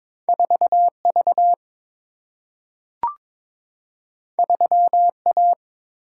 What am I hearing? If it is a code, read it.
Morse code, heard as a keyed tone, about 22 wpm, 700 Hz. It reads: 4V 3A
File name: practice / 22wpm / Two Letter-Number Combinations - 22wpm